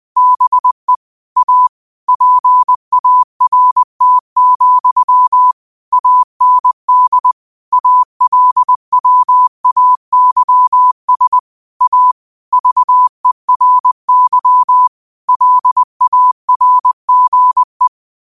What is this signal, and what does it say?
Morse code, heard as a keyed tone, about 20 wpm, 1000 Hz. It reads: BE A PART, AND ALWAYS A VERY LARGE
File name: qotdfiles / 20wpm